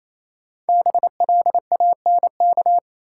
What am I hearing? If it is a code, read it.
Morse code, heard as a keyed tone, about 28 wpm, 700 Hz. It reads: BLANK
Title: Sentences from Top 1500 Words - New Word Drill - 28wpm